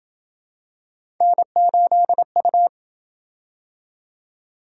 Morse code, heard as a keyed tone, about 27 wpm, 700 Hz. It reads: N8U